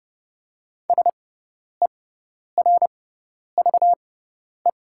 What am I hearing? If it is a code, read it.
Morse code, heard as a keyed tone, about 31 wpm, spaced as for 10 wpm, 700 Hz. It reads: SERVE
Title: 6x Character Spacing - Top 500 Words - 30wpm